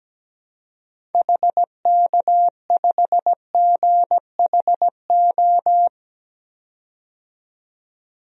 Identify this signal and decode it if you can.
Morse code, heard as a keyed tone, about 17 wpm, 700 Hz. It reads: HK5GHO
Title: DXCC-100 Call Signs - 6 Characters - 17wpm